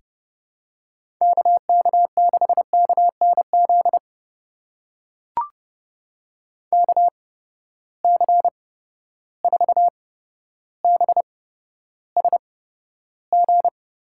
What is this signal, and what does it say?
Morse code, heard as a keyed tone, about 31 wpm, spaced as for 13 wpm, 700 Hz. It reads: KK6KNZ KC4BSG
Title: USA Call Signs - 6 Characters - 8x Character Spacing - 30wpm